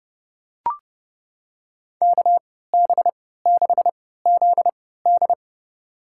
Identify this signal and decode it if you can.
Morse code, heard as a keyed tone, about 30 wpm, spaced as for 17 wpm, 700 Hz. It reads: KB6ZD
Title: USA Call Signs - 3x Character Spacing - 30wpm